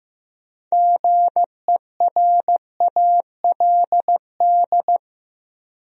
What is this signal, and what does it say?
Morse code, heard as a keyed tone, about 15 wpm, 700 Hz. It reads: GERALD
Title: USA Common Names - 15wpm